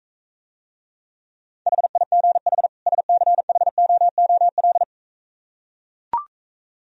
Morse code, heard as a keyed tone, about 44 wpm, 690 Hz. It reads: HIGH SCHOOL